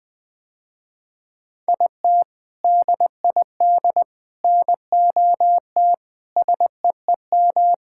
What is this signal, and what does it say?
Morse code, heard as a keyed tone, about 20 wpm, 700 Hz. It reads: IT DID NOT SEEM
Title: Sentences with 4-Word Phrases - Spoken Beginning - 20wpm